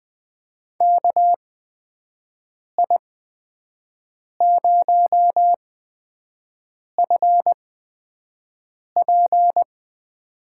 Morse code, heard as a keyed tone, about 20 wpm, spaced as for 5 wpm, 700 Hz. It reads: KI0FP
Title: USA Call Signs - 8x Character Spacing - 20wpm